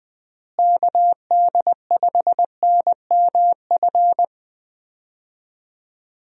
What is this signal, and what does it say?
Morse code, heard as a keyed tone, about 20 wpm, 700 Hz. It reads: KD5NMF
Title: USA Call Signs - 6 Characters - Repeated 3x - 20wpm